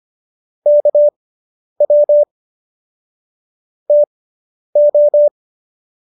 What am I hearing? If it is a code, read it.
Morse code, heard as a keyed tone, about 25 wpm, spaced as for 10 wpm, 590 Hz. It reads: KW TO